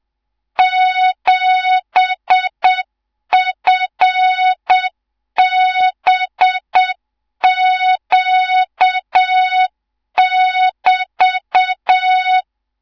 Morse code, heard as a keyed tone, about 7 wpm, 3700 Hz, with noise. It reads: 7FBQ<BT>
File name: Morze-HA2MN / hang / 18 IRASJELEK2